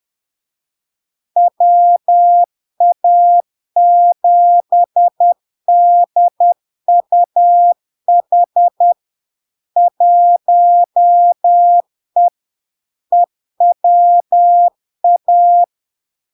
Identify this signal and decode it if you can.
Morse code, heard as a keyed tone, about 10 wpm, 700 Hz. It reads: WA7DUH 1E EWA
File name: LARK-FD-2022-Log-in-CW-10wpm-700-Hz